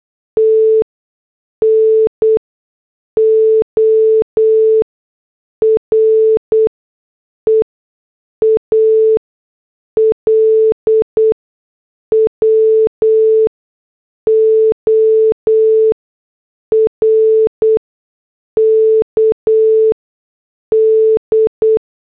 Morse code, heard as a keyed tone, about 8 wpm, spaced as for 6 wpm, 440 Hz. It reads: T NO REAL WORK D